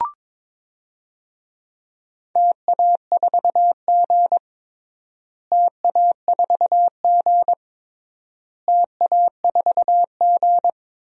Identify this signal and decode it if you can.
Morse code, heard as a keyed tone, about 22 wpm, 700 Hz. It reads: TA4G TA4G TA4G
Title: DXCC-100 Call Signs - 4 Characters - Repeated 3x - 22wpm